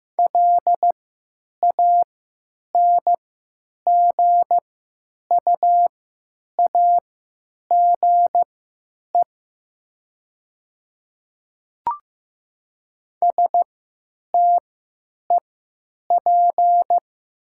Morse code, heard as a keyed tone, about 15 wpm, spaced as for 9 wpm, 700 Hz. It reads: LANGUAGE STEP
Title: 3x Character Spacing - Top 500 Words - 15wpm